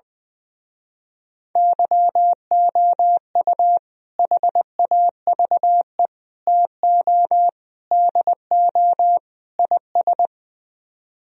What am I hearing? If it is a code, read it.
Morse code, heard as a keyed tone, about 21 wpm, 700 Hz. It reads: YOU HAVE TO DO IS
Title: Speed-Racing - Sets of 5 Words 20wpm